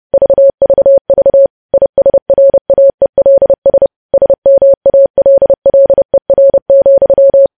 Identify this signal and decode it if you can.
Morse code, heard as a keyed tone, about 30 wpm, 550 Hz. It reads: VVV ISRAELS SMALLER,
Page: news headlines in morse code at 30 wpm